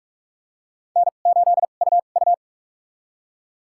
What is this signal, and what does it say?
Morse code, heard as a keyed tone, about 46 wpm, spaced as for 32 wpm, 700 Hz. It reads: N8UU
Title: USA Call Signs - 4 Characters - 2x Character Spacing - 45wpm